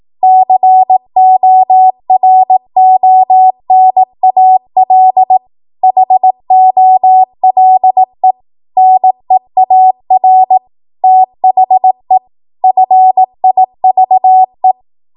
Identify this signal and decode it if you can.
Morse code, heard as a keyed tone, about 18 wpm, 750 Hz. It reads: CORONAL HOLE NEAR THE FIVE
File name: Archive / 18 WPM / 18WPM